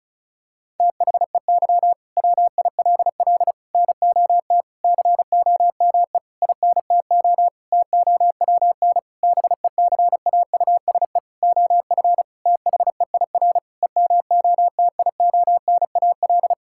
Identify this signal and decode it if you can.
Morse code, heard as a keyed tone, about 34 wpm, 700 Hz. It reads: THEY WILL NOT COME INTO TOWN BECAUSE OF THEIR EMOTIONAL
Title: Sentences from Top 1000 Words - Review Part 5 - 35wpm